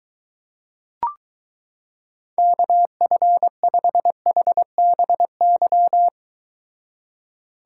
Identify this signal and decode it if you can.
Morse code, heard as a keyed tone, about 22 wpm, 700 Hz. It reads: KF5HBY